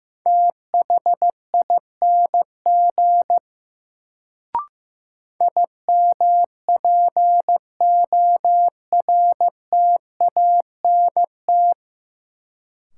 Morse code, heard as a keyed tone, about 15 wpm, 700 Hz. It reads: THING IMPORTANT
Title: Sets of 3 Words - New Word Drill - 15wpm